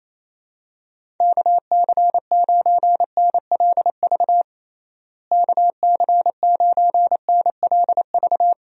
Morse code, heard as a keyed tone, about 28 wpm, 700 Hz. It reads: KC9NLV KC9NLV
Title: USA Call Signs - 6 Characters - Repeated 3x - 28wpm